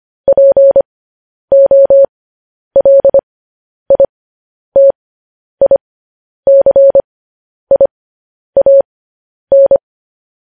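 Morse code, heard as a keyed tone, about 25 wpm, spaced as for 10 wpm, 550 Hz. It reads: POLITICIAN
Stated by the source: Morse code news headlines at 10 wpm.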